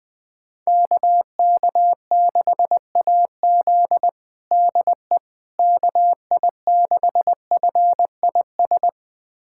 Morse code, heard as a keyed tone, about 21 wpm, 700 Hz. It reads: KK6AZ DE KI6FIS
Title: USA Callsign DE Callsign 20wpm